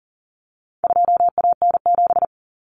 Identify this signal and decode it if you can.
Morse code, heard as a keyed tone, about 40 wpm, 700 Hz. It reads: 2AN7
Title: Common Cut Numbers - 4-Digit 40wpm